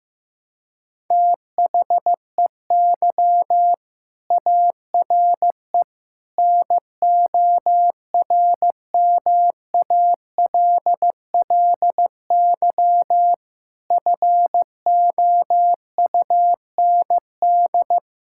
Morse code, heard as a keyed tone, about 15 wpm, 700 Hz. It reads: THEY ARE NORMALLY FOUND
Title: Sentences from Top 1000 Words - Review Part 6 - 15wpm